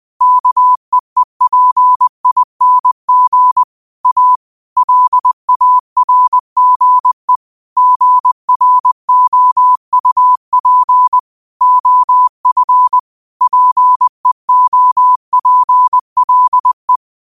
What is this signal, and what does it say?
Morse code, heard as a keyed tone, about 20 wpm, 1000 Hz. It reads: KEEPING A LARGE GROUP OF PEOPLE
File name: qotdfiles / 20wpm